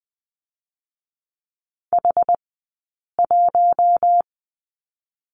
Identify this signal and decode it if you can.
Morse code, heard as a keyed tone, about 20 wpm, spaced as for 8 wpm, 700 Hz. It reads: H1
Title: 080 Lesson 14d - Number 2 - Two Characters 20wpm